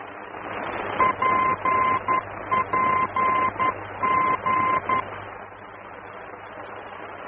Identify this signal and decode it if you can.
Morse code, heard as a keyed tone, about 11 wpm, 1040 Hz, with noise. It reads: PPG
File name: france / mx / PPG-F-VOR